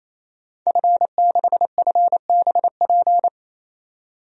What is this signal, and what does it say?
Morse code, heard as a keyed tone, about 28 wpm, 700 Hz. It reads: F6FBP
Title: DXCC-100 Call Signs - 5 Characters - 28wpm